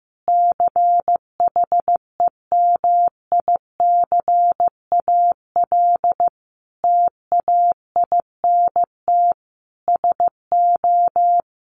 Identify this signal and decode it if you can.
Morse code, heard as a keyed tone, about 15 wpm, 700 Hz. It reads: CHEMICAL TAINT SO
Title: A Sound Of Thunder 15wpm